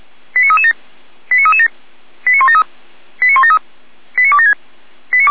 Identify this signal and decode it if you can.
Five-tone selective call (ZVEI), heard as quick runs of five tones, tones of about 70 ms. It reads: 89337 89337 89173 89173 89276
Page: /455/SDR/ZVEI/89078/